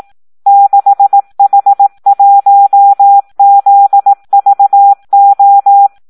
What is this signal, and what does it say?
Morse code, heard as a keyed tone, about 18 wpm, 800 Hz. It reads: 6H1ZVO
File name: Audio practica CW / SD 1 18